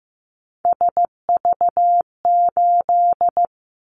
Morse code, heard as a keyed tone, about 15 wpm, 700 Hz. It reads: SV8
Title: Callsign Prefixes for Top 50 DXCC - 15wpm